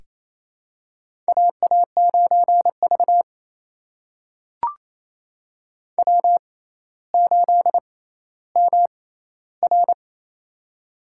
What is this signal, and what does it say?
Morse code, heard as a keyed tone, about 28 wpm, spaced as for 14 wpm, 700 Hz. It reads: AA9V W8MR
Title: USA Call Signs - 4 Characters - 6x Character Spacing - 28wpm